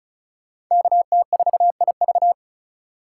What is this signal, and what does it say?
Morse code, heard as a keyed tone, about 45 wpm, 700 Hz. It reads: KT4IV